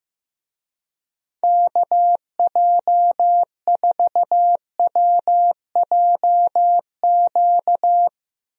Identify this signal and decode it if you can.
Morse code, heard as a keyed tone, about 15 wpm, 700 Hz. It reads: KJ4WJQ
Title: USA Call Signs - 6 Characters - Repeated 2x - 15wpm